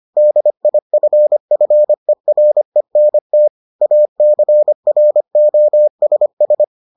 Morse code, heard as a keyed tone, about 25 wpm, 600 Hz. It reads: DIFFERENT ACROSS